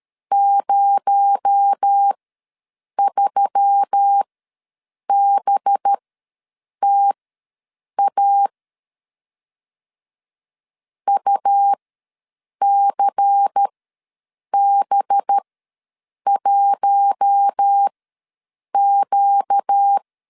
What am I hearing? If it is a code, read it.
Morse code, heard as a keyed tone, about 13 wpm, spaced as for 7 wpm, 800 Hz. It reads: 03BTA UCB1Q
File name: LEZIONI / 7 GF9Z / 7 GF9 ws 12wpm 3344